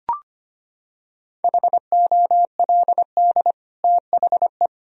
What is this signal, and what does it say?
Morse code, heard as a keyed tone, about 25 wpm, 700 Hz. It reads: HOLD THE
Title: Sentences from Top 1000 Words - Review Part 2 - 25wpm